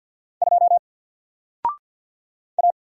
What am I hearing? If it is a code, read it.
Morse code, heard as a keyed tone, about 50 wpm, 700 Hz. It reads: 2 A
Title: Single Letter-Number 50wpm